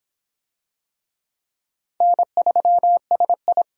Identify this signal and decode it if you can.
Morse code, heard as a keyed tone, about 24 wpm, 700 Hz. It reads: N3SI